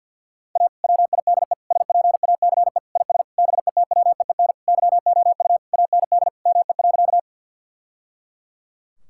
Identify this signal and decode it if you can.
Morse code, heard as a keyed tone, about 47 wpm, 700 Hz. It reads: A WIDE SPACE IS BETWEEN YOU AND ME.
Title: Sentences from Top 1000 Words - Review Part 6 - 50wpm